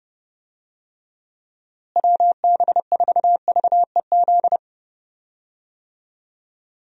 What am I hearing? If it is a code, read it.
Morse code, heard as a keyed tone, about 30 wpm, 700 Hz. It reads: WB4VEZ
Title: USA Call Signs - 6 Characters - Repeated 2x - 30wpm